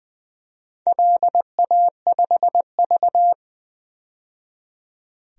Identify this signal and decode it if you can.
Morse code, heard as a keyed tone, about 20 wpm, 700 Hz. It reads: LA5V
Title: DXCC-100 Call Signs - 4 Characters - 20wpm